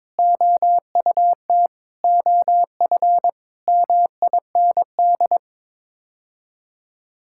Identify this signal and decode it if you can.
Morse code, heard as a keyed tone, about 22 wpm, 700 Hz. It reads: OUT OF MIND
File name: practice / 22wpm / Sets of 7 Words - 22wpm